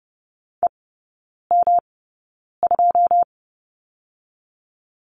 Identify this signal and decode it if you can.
Morse code, heard as a keyed tone, about 30 wpm, spaced as for 9 wpm, 700 Hz. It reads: EM2
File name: Lesson 18e - Letter m - Three Characters 30wpm